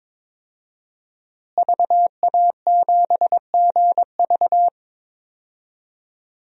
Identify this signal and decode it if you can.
Morse code, heard as a keyed tone, about 22 wpm, 700 Hz. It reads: VA7GV
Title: DXCC-100 Call Signs - 5 Characters - Repeated 2x - 22wpm